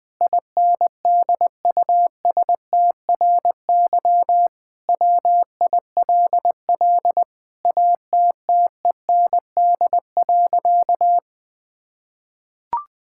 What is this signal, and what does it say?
Morse code, heard as a keyed tone, about 21 wpm, 700 Hz. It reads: INDUSTRY WILL ATTEND.
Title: Sentences with 7-Word Phrases - 20wpm